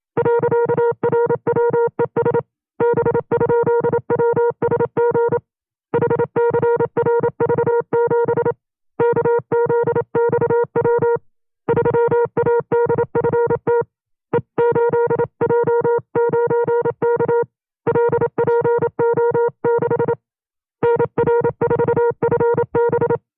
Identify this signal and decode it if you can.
Morse code, heard as a keyed tone, about 27 wpm, 470 Hz. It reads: .RWES B?WSG HCRV7 KZXW 3ADFT E8J9K LPO6 NR4FB
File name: 25WPM